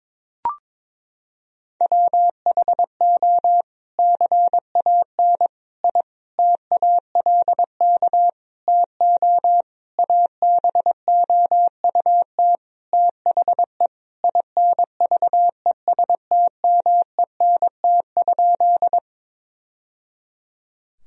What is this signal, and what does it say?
Morse code, heard as a keyed tone, about 21 wpm, 700 Hz. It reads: WHO CAN I TALK TO ABOUT THE INVESTMENT?
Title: Sentences from Top 1000 Words - Review Part 4 - 22wpm